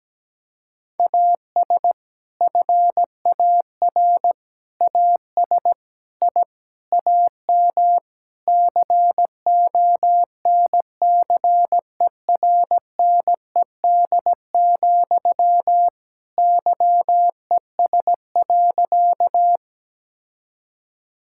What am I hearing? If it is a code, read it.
Morse code, heard as a keyed tone, about 17 wpm, 700 Hz. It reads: AS FAR AS I AM CONCERNED, YES.
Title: Sentences from Top 1000 Words - Review Part 6 - 17wpm